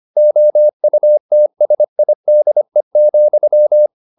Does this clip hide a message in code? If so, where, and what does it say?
Yes: Morse code from 0.2 to 3.9 s, heard as a keyed tone, about 25 wpm, 600 Hz. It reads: OUTSIDE,